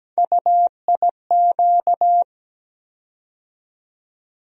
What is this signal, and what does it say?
Morse code, heard as a keyed tone, about 17 wpm, 700 Hz. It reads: UIQ
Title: USA Call Sign Postfixes - 3 Letters 17wpm